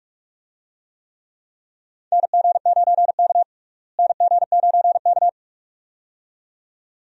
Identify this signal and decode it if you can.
Morse code, heard as a keyed tone, about 42 wpm, 700 Hz. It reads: NG9K NG9K
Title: USA Call Signs - 4 Characters - Repeated 2x - 45wpm